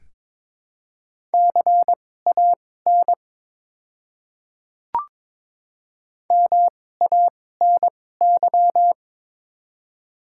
Morse code, heard as a keyed tone, about 22 wpm, spaced as for 16 wpm, 700 Hz. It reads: CAN MANY
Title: 2x Character Spacing - Top 500 Words - 22wpm